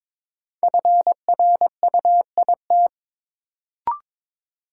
Morse code, heard as a keyed tone, about 22 wpm, 700 Hz. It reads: FRUIT
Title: Sentences from Top 1300 Words - New Word Drill Alt - 22wpm